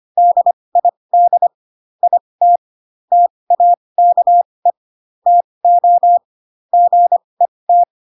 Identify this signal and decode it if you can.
Morse code, heard as a keyed tone, about 25 wpm, spaced as for 20 wpm, 700 Hz. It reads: DID IT TAKE TO GET
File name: Pinocchio / Pinocchio ch4 15 25 ik1hge